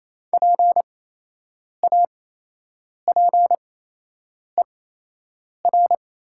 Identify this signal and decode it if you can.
Morse code, heard as a keyed tone, about 28 wpm, spaced as for 8 wpm, 700 Hz. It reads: PAPER
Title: 8x Character Spacing - Top 300 Words - 28wpm